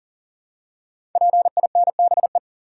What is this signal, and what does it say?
Morse code, heard as a keyed tone, about 42 wpm, 700 Hz. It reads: PINBE
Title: Code Groups - 5 Letter:Numbers - 40wpm